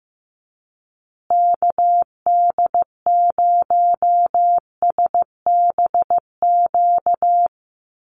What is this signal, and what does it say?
Morse code, heard as a keyed tone, about 15 wpm, 700 Hz. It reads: KD0SBQ
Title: USA Call Signs Repeated 3x 15wpm